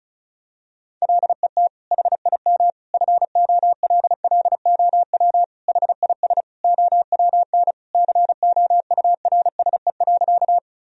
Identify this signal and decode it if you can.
Morse code, heard as a keyed tone, about 36 wpm, 700 Hz. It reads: LET HIM FOLLOW HIS OWN COURSE.
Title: Sentences from Top 1000 Words - Review Part 1 - 35wpm